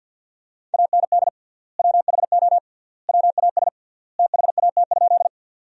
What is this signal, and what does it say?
Morse code, heard as a keyed tone, about 49 wpm, 700 Hz. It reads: AND WHO WAS THAT?
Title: 3x Word-Spacing - Sentences from Top 100 Words - 50wpm